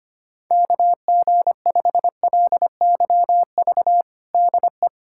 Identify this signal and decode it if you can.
Morse code, heard as a keyed tone, about 25 wpm, 700 Hz. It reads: KG5LYV DE
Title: USA Callsign DE Callsign 25wpm